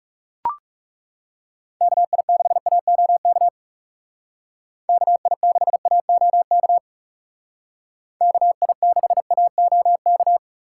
Morse code, heard as a keyed tone, about 39 wpm, 700 Hz. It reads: KI6AOK KI6AOK KI6AOK